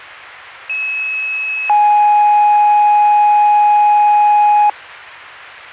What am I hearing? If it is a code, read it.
Two-tone page, heard as a one-second tone, then a three-second tone, about 2600 then 830 Hz. It is heard from 0.7 to 4.7 s.